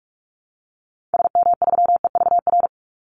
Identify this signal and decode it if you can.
Morse code, heard as a keyed tone, about 44 wpm, 700 Hz. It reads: SM3EVR
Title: Call Signs - 45wpm